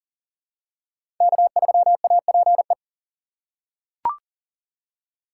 Morse code, heard as a keyed tone, about 40 wpm, 700 Hz. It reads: K3APE
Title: USA Call Signs - 5 Characters - 2x Character Spacing - 40wpm